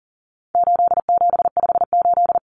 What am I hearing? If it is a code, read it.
Morse code, heard as a keyed tone, about 40 wpm, 700 Hz. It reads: 8758
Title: Numbers - 4-Digit 40wpm